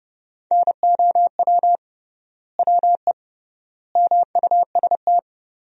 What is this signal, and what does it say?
Morse code, heard as a keyed tone, about 30 wpm, 700 Hz. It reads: NOW WE MUST
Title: 3x Word-Spacing - Sentences from Top 500 Words - 30wpm